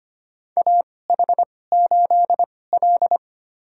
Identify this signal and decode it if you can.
Morse code, heard as a keyed tone, about 25 wpm, spaced as for 18 wpm, 700 Hz. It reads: AH8L
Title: USA Call Signs - 4 Characters - 2x Character Spacing - 25wpm